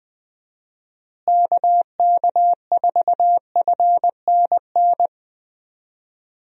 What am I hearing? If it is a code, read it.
Morse code, heard as a keyed tone, about 20 wpm, 700 Hz. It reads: KK4FNN
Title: USA Call Signs - 6 Characters - Repeated 2x - 20wpm